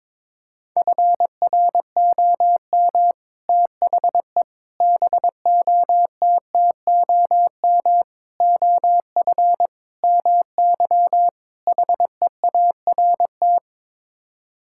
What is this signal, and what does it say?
Morse code, heard as a keyed tone, about 22 wpm, 700 Hz. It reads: FROM THE BOTTOM OF MY HEART
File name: practice / 22wpm / Sets of 6 Words - 22wpm